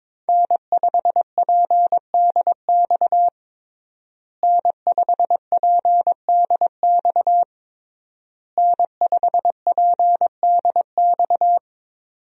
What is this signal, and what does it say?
Morse code, heard as a keyed tone, about 22 wpm, 700 Hz. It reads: N5PDX N5PDX N5PDX
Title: USA Call Signs - 5 Characters - Repeated 3x - 22wpm